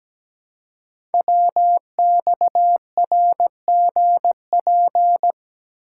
Morse code, heard as a keyed tone, about 17 wpm, 700 Hz. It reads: WXRGP
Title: Code Groups - 5 Letters - 17wpm